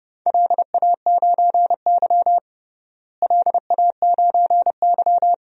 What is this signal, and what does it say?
Morse code, heard as a keyed tone, about 30 wpm, 700 Hz. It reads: LA9Y LA9Y
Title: DXCC-100 Call Signs - 4 Characters - Repeated 2x - 30wpm